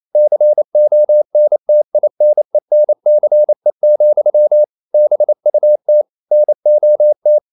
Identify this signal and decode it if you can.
Morse code, heard as a keyed tone, about 28 wpm, 600 Hz. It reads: CONTINENCE, BUT NOT
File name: CW / Practice / EnglishQuotations0128